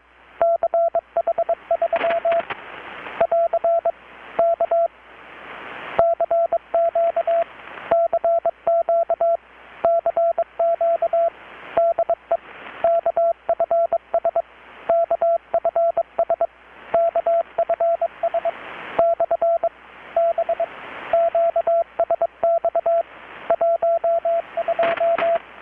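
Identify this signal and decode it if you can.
Morse code, heard as a keyed tone, about 22 wpm, 670 Hz, with radio noise. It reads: CH3 <AR> K CQ CQ CQ DE KFS KFS KFS / B QSX 12
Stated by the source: Morse from KPH's Night of Nights XX maritime radiotelegraphy event, received in Chiapas, Mexico about 3,800 km away on a Kenwood R-600 with a grounded 10 m long-wire antenna.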